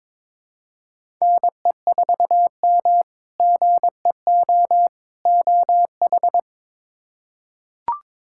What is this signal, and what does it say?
Morse code, heard as a keyed tone, about 22 wpm, 700 Hz. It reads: NE4M GEO OH
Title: K1USN SST QSO Practice - 22wpm